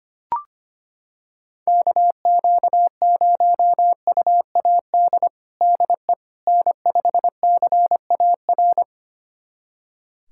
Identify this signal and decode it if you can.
Morse code, heard as a keyed tone, about 25 wpm, 700 Hz. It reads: KQ0UAD DE N5CAR
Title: USA Callsign DE Callsign 25wpm